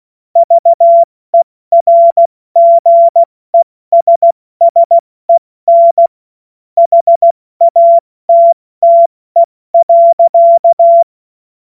Morse code, heard as a keyed tone, about 16 wpm, 680 Hz. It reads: VERGESSEN HATTE.
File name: Morsekurs / Witze / Witz 13 Februar 2024 70 F80